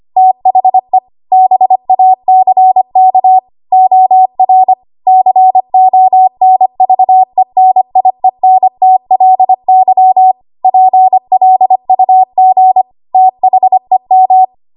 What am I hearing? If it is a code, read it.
Morse code, heard as a keyed tone, about 25 wpm, 750 Hz. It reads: THE BACK OR CONVENIENTLY PLUG THEM
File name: Archive / 25 WPM / 25WPM